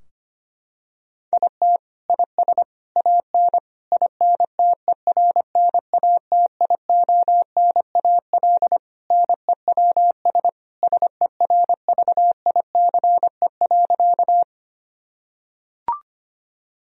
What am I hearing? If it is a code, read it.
Morse code, heard as a keyed tone, about 25 wpm, 700 Hz. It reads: IT IS AN INTERNATIONAL NEWS SERVICE.
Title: Sentences from Top 1000 Words - Review Part 4 - 25wpm